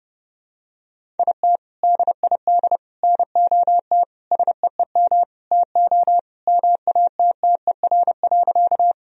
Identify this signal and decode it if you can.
Morse code, heard as a keyed tone, about 30 wpm, 700 Hz. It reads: IT DID NOT SEEM TO MATTER.
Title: Sentences from Top 1000 Words - Review Part 1 - 30wpm